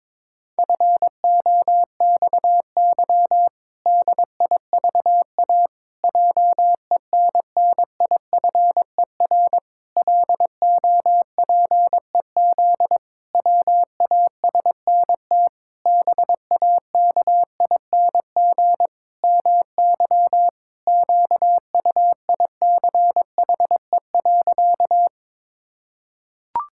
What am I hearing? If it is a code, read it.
Morse code, heard as a keyed tone, about 22 wpm, 700 Hz. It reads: FOXY DIVA JENNIFER LOPEZ WASNT BAKING MY QUICHE.
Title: Pangram Challenge v2 - 22wpm